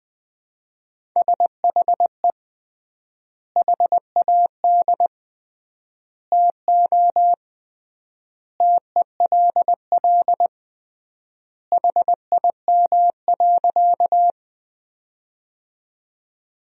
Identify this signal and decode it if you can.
Morse code, heard as a keyed tone, about 20 wpm, 700 Hz. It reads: SHE HAD TO TELL HIM.
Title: 3x Word-Spacing - Sentences from Top 200 Words - 20wpm